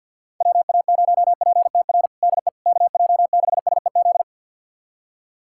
Morse code, heard as a keyed tone, about 52 wpm, 690 Hz. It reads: WA0PTR DE KJ6SEZ
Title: USA Callsign DE Callsign - 6 Characters - 50wpm